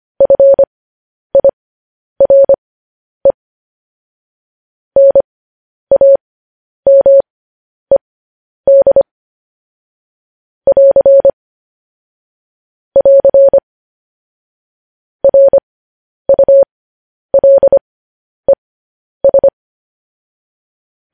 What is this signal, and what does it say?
Morse code, heard as a keyed tone, about 25 wpm, spaced as for 10 wpm, 550 Hz. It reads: FIRE NAMED <AR> <AR> RULES
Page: news headlines in morse code at 10 wpm